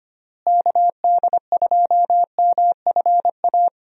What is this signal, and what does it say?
Morse code, heard as a keyed tone, about 25 wpm, 700 Hz. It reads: KD2MFA
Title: USA Call Signs - 6 Characters - Repeated 3x - 25wpm